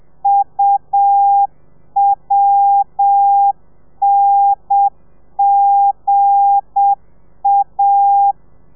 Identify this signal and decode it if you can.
Morse code, heard as a keyed tone, about 7 wpm, 800 Hz. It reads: UWNGA
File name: telegrafia / i2tfj / Lez8 35